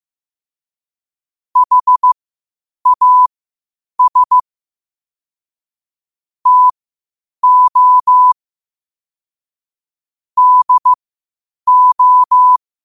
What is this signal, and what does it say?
Morse code, heard as a keyed tone, about 15 wpm, spaced as for 8 wpm, 1000 Hz. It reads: HAS TO DO
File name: 8wpm